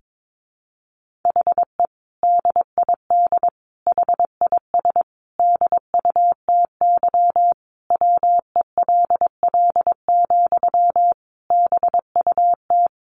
Morse code, heard as a keyed tone, about 22 wpm, 700 Hz. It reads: HE DID HIS DUTY WELL, BUT
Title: Sentences from Top 1500 Words 22wpm